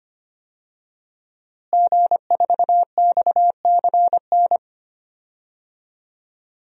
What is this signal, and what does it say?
Morse code, heard as a keyed tone, about 25 wpm, 700 Hz. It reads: G4XCN
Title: DXCC-100 Call Signs - 5 Characters - Repeated 2x - 25wpm